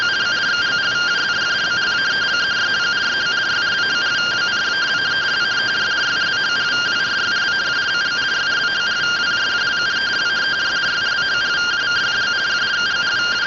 Two-tone teleprinter signal, about 170 Hz shift.